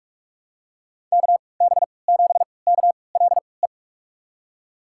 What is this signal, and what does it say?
Morse code, heard as a keyed tone, about 46 wpm, spaced as for 25 wpm, 700 Hz. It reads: KB7KLE
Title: USA Call Signs - 6 Characters - 3x Character Spacing - 45wpm